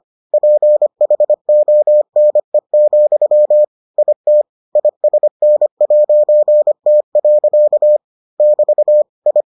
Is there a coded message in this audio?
Yes: Morse code from 0.3 to 9.4 s, heard as a keyed tone, about 25 wpm, 600 Hz. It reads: PHONE, IT ISN'T. <BT> I